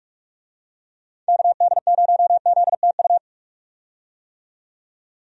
Morse code, heard as a keyed tone, about 46 wpm, 700 Hz. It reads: KD0ZTU
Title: USA Call Signs - 6 Characters - 45wpm